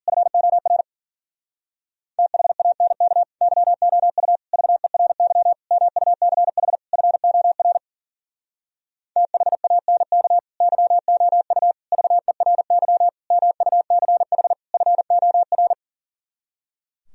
Morse code, heard as a keyed tone, about 46 wpm, 700 Hz. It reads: FOR THANK YOU VERY MUCH FOR THANK YOU VERY MUCH FOR
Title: Speed-Racing - Sets of 5 Words 40wpm